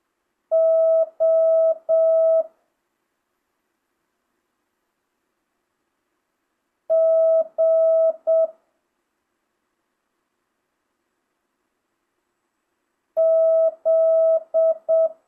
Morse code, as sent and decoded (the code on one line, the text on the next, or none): --- / --. / --..
O G Z